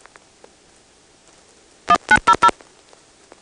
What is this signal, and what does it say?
DTMF, heard as telephone touch tones, tones of about 70 ms. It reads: 1900
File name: phonetrips / PrinceWmCoVA5HQ